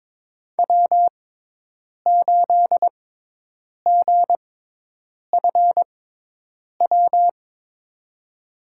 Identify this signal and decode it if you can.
Morse code, heard as a keyed tone, about 21 wpm, spaced as for 8 wpm, 700 Hz. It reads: W8GFW
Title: USA Call Signs - 5 Characters - 6x Character Spacing - 22wpm